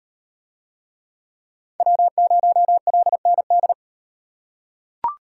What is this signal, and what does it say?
Morse code, heard as a keyed tone, about 37 wpm, 700 Hz. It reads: W0LND